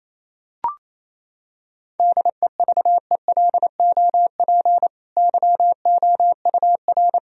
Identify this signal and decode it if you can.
Morse code, heard as a keyed tone, about 27 wpm, 700 Hz. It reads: DEVELOP YOUR
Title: Sentences from Top 1000 Words - Review Part 2 - 28wpm